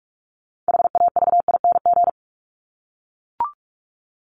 Morse code, heard as a keyed tone, about 44 wpm, 700 Hz. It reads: HAVING
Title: ING Suffix - Top 100 Words with Base Words 45wpm